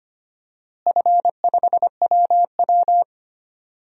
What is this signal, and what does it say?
Morse code, heard as a keyed tone, about 25 wpm, 700 Hz. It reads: F5WW
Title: DXCC-100 Call Signs - 4 Characters - Repeated 3x - 25wpm